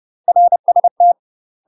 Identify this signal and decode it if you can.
Morse code, heard as a keyed tone, about 30 wpm, 700 Hz. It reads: RST
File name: UK MASTER QSOs TRAINING / 30WPM / UK MASTER QSOs 0030